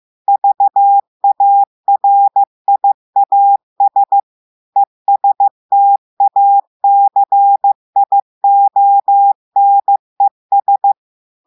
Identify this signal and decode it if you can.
Morse code, heard as a keyed tone, about 15 wpm, 800 Hz. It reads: VARIAS ESTACIONES
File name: eh1ard-amateur-radio-day-2016 mf feed 1